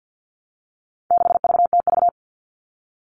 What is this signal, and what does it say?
Morse code, heard as a keyed tone, about 50 wpm, 700 Hz. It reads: 6VTV
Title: All Cut Numbers - 4-Digit 50wpm